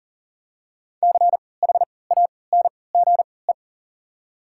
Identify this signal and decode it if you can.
Morse code, heard as a keyed tone, about 40 wpm, spaced as for 23 wpm, 700 Hz. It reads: CHANGE
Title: 3x Character Spacing - Top 200 Words - 40wpm